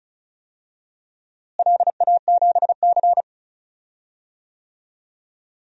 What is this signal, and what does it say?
Morse code, heard as a keyed tone, about 34 wpm, 700 Hz. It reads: LA7C
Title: DXCC-100 Call Signs - 4 Characters - Repeated 3x - 35wpm